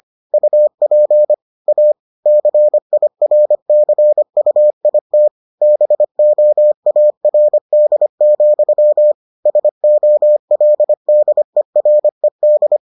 Morse code, heard as a keyed tone, about 25 wpm, 600 Hz. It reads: UP A CIRCUIT BOARD, SOLDERED